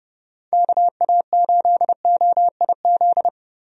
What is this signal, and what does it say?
Morse code, heard as a keyed tone, about 29 wpm, 700 Hz. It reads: KA8OIZ
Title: USA Call Signs - 6 Characters - Repeated 3x - 30wpm